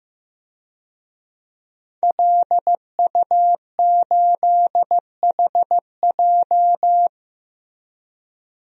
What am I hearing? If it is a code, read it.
Morse code, heard as a keyed tone, about 15 wpm, 700 Hz. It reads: LU8HJ
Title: DXCC-100 Call Signs - 5 Characters - Repeated 3x - 15wpm